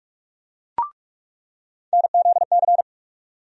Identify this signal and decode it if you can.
Morse code, heard as a keyed tone, about 47 wpm, 700 Hz. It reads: NZC
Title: USA Call Sign Postfixes - 3 Letters 45wpm